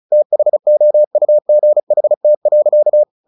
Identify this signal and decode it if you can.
Morse code, heard as a keyed tone, about 34 wpm, 600 Hz. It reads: THOUGHT.
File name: the old man and the sea 2 35wpm